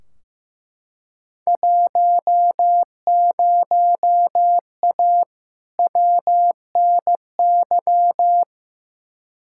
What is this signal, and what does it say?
Morse code, heard as a keyed tone, about 15 wpm, 700 Hz. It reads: 10A WNY
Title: Field Day Exchanges - 15wpm